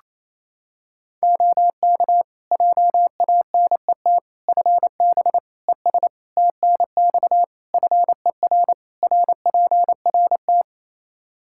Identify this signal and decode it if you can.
Morse code, heard as a keyed tone, about 27 wpm, 700 Hz. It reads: OK JANET FB ES TNX FER RPRT
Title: Practice QSO - Exchange 2 - Info - 28wpm